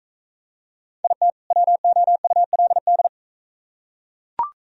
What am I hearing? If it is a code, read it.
Morse code, heard as a keyed tone, about 43 wpm, 700 Hz. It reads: IT WOULD